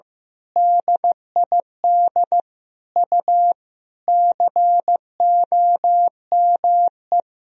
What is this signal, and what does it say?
Morse code, heard as a keyed tone, about 16 wpm, 700 Hz. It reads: DID U COME
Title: Sentences from Top 100 Words and Abbreviations 15wpm